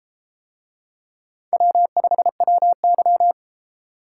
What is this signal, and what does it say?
Morse code, heard as a keyed tone, about 33 wpm, 700 Hz. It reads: W5WY